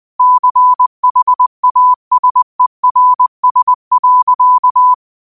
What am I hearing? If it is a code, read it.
Morse code, heard as a keyed tone, about 20 wpm, 1000 Hz. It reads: CHASERS.
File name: qotdfiles / 20wpm